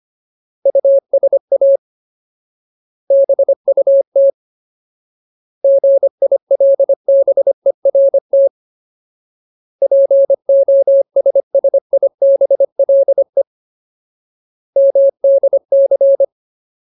Morse code, heard as a keyed tone, about 25 wpm, 550 Hz. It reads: USA BUT GILBERT POSSIBLE MDC
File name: Ultimate Word List / 25WPM Ultimate